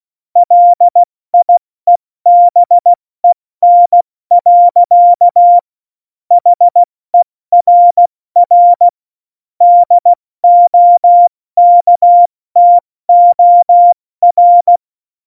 Morse code, heard as a keyed tone, about 16 wpm, 700 Hz. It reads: LIEBEN. HERR DOKTOR
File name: Morsekurs / Witze / Witz 05 Mai, 2020 70 F80